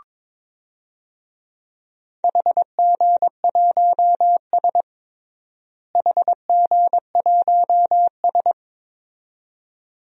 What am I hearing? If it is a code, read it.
Morse code, heard as a keyed tone, about 22 wpm, 700 Hz. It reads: HG1S HG1S
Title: DXCC-100 Call Signs - 4 Characters - Repeated 2x - 22wpm